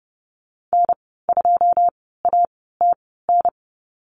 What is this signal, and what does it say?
Morse code, heard as a keyed tone, about 30 wpm, spaced as for 17 wpm, 700 Hz. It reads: N2ATN
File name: Lesson 15g - Number 5 - Call Signs 30wpm